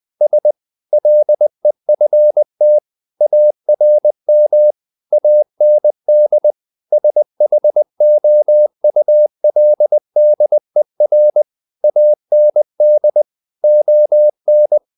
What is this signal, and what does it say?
Morse code, heard as a keyed tone, about 20 wpm, 600 Hz. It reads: S LEFT ARM AND SHOULDER AND ON
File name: the old man and the sea / the old man and the sea part 1 20wpm